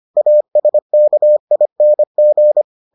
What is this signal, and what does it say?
Morse code, heard as a keyed tone, about 25 wpm, 600 Hz. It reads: ASKING